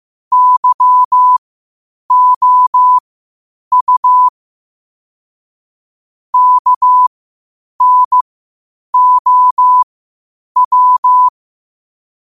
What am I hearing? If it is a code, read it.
Morse code, heard as a keyed tone, about 15 wpm, spaced as for 8 wpm, 1000 Hz. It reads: YOU KNOW
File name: qotdfiles / 8wpm